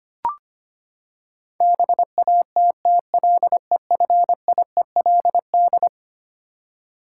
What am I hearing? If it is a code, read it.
Morse code, heard as a keyed tone, about 25 wpm, 700 Hz. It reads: BATTLEFIELD
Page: morse-code-ninja-practice-25wpm